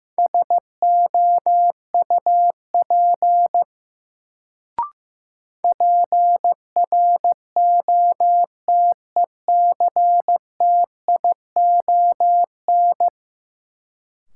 Morse code, heard as a keyed tone, about 15 wpm, 700 Hz. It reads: SOUP PROTECTION
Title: Sentences from Top 1600 Words - New Word Drill Alt - 15wpm